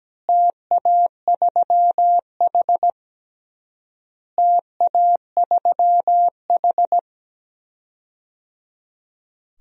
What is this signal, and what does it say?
Morse code, heard as a keyed tone, about 17 wpm, 700 Hz. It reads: TA3H TA3H
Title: DXCC-100 Call Signs - 4 Characters - Repeated 3x - 17wpm